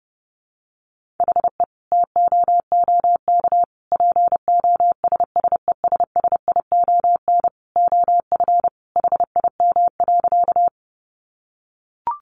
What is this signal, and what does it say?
Morse code, heard as a keyed tone, about 30 wpm, 700 Hz. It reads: HE TOOK POSSESSION OF HIM.
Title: Sentences from Top 1400 Words 30wpm